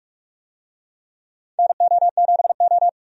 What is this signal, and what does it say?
Morse code, heard as a keyed tone, about 42 wpm, 700 Hz. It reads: NO7O
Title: USA Call Signs - 4 Characters - 45wpm